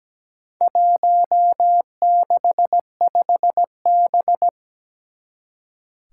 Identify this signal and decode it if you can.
Morse code, heard as a keyed tone, about 17 wpm, 700 Hz. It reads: 165B
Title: Four Challenging Letter-Number Combinations 17wpm